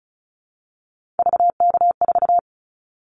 Morse code, heard as a keyed tone, about 36 wpm, 700 Hz. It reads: VK4
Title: Callsign Prefixes for Top 50 DXCC - 35wpm